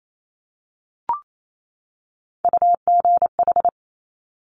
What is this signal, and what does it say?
Morse code, heard as a keyed tone, about 28 wpm, 700 Hz. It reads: UGH
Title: Top Three Letter Combinations in Words 28wpm